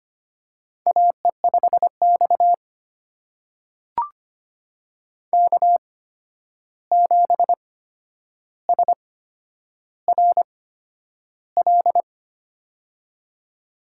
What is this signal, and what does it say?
Morse code, heard as a keyed tone, about 25 wpm, spaced as for 10 wpm, 700 Hz. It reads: AE5X K7SRL
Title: USA Call Signs - 8x Character Spacing - 25wpm